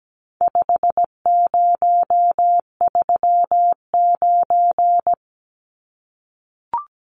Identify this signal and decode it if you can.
Morse code, heard as a keyed tone, about 17 wpm, 700 Hz. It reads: 5039
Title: Numbers - 4-Digit 17wpm